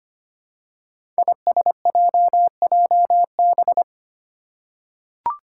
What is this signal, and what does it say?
Morse code, heard as a keyed tone, about 25 wpm, 700 Hz. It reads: ISJJB
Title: Five Challenging Letter-Number Combinations 25wpm